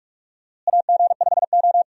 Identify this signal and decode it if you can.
Morse code, heard as a keyed tone, about 46 wpm, 700 Hz. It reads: AG5O
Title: USA Call Signs - 4 Characters - Repeated 2x - 45wpm